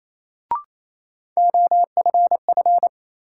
Morse code, heard as a keyed tone, about 28 wpm, 700 Hz. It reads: OFF
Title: Sentences from Top 200 Words - New Word Drill - 28wpm